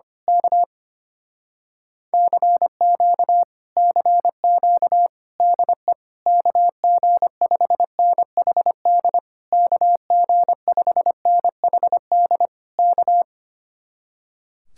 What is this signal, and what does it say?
Morse code, heard as a keyed tone, about 24 wpm, 700 Hz. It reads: K CQ CQ DE KG5NHD KG5NHD K
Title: Speed-Racing Common QSO Phrases - 25wpm